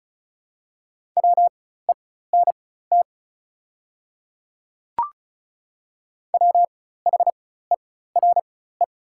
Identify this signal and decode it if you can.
Morse code, heard as a keyed tone, about 34 wpm, spaced as for 16 wpm, 690 Hz. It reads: WENT WHERE
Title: 4x Character Spacing - Top 200 Words - 35wpm